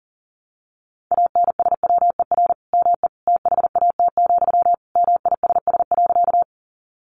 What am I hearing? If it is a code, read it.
Morse code, heard as a keyed tone, about 40 wpm, 700 Hz. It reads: ANSWER ME THAT, MISS.
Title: Sentences from Top 500 Words 40wpm